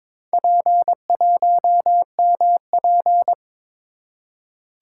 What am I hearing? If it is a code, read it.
Morse code, heard as a keyed tone, about 22 wpm, 700 Hz. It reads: P1MP
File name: practice / 22wpm / Four Letter-Number Combinations - 22wpm